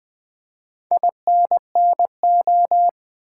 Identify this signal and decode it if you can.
Morse code, heard as a keyed tone, about 20 wpm, 700 Hz. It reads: INNO
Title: Build A Word - 4 Letters At A Time - 10-Letter Words - 20wpm